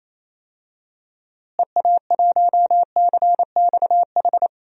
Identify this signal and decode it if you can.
Morse code, heard as a keyed tone, about 28 wpm, 700 Hz. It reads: EA1CXH
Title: DXCC-100 Call Signs - 6 Characters - Repeated 3x - 28wpm